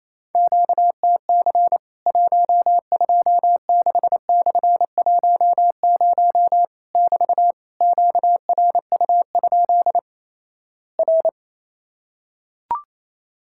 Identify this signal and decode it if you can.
Morse code, heard as a keyed tone, about 28 wpm, 700 Hz. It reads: QTC 126/10 <BT> QRU? R
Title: Worked All Europe DX Contest - QTC Exchange - 28wpm